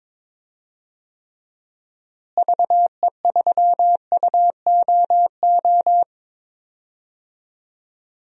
Morse code, heard as a keyed tone, about 22 wpm, 700 Hz. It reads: VE3UOO
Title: DXCC-100 Call Signs - 6 Characters - Repeated 3x - 22wpm